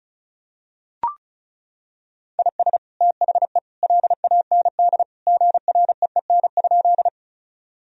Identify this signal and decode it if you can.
Morse code, heard as a keyed tone, about 36 wpm, 700 Hz. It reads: IS THE LAND GREEN?
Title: Sentences from Top 1000 Words - Review Part 2 - 35wpm